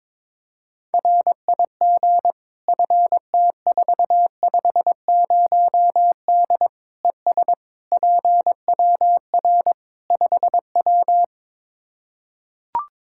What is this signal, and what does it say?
Morse code, heard as a keyed tone, about 22 wpm, 700 Hz. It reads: RIG FT450D ES PWR 5W
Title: Practice QSO - Exchange 2 - Info - 22wpm